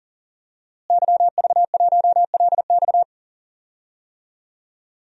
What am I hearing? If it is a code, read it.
Morse code, heard as a keyed tone, about 40 wpm, 700 Hz. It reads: YV1LX
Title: DXCC-100 Call Signs - 5 Characters - Repeated 2x - 40wpm